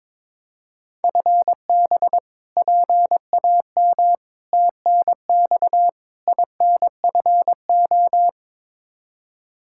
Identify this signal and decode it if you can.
Morse code, heard as a keyed tone, about 22 wpm, 700 Hz. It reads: FB PAM TNX INFO
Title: Practice QSO - Exchange 3 - More Info - 22wpm